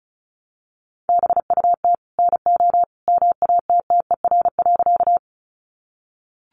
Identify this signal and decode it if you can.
Morse code, heard as a keyed tone, about 34 wpm, 700 Hz. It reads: BUT NO MATTER.
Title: Sentences from Top 700 Words 35wpm